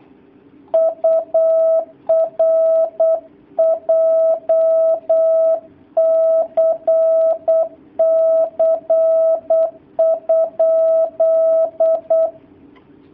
Morse code, heard as a keyed tone, about 8 wpm, 650 Hz. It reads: UR JCC?
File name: jh8cbh / 2019cw / sp08- w